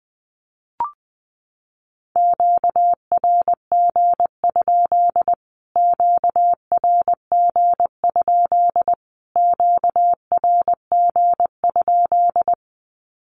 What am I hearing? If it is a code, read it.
Morse code, heard as a keyed tone, about 19 wpm, 700 Hz. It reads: QRG? QRG? QRG?
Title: Q-Codes Repeated 3x 20wpm